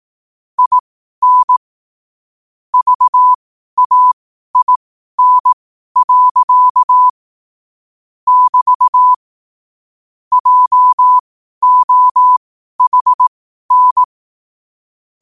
Morse code, heard as a keyed tone, about 18 wpm, spaced as for 12 wpm, 1000 Hz. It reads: IN VAIN. <BT> JOHN
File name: qotdfiles / 12wpm